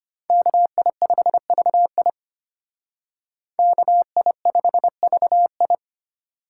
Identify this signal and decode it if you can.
Morse code, heard as a keyed tone, about 32 wpm, 700 Hz. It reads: KI5VI KI5VI